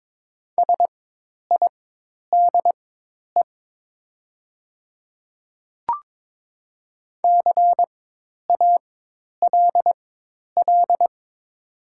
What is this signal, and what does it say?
Morse code, heard as a keyed tone, about 23 wpm, spaced as for 10 wpm, 700 Hz. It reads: SIDE CALL
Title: 4x Character Spacing - Top 100 Words - 22wpm